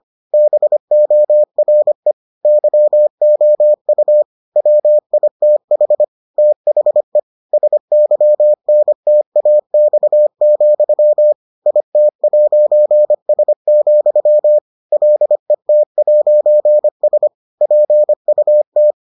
Morse code, heard as a keyed tone, about 25 wpm, 600 Hz. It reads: BORE YOU WITH THE SYNTAX, IT'S, LET'S PUT